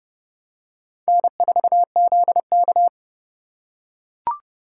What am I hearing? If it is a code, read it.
Morse code, heard as a keyed tone, about 28 wpm, 700 Hz. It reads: N4ZK